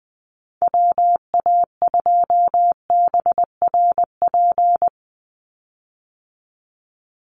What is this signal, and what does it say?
Morse code, heard as a keyed tone, about 19 wpm, 700 Hz. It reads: WA2BRP
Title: USA Call Signs 20wpm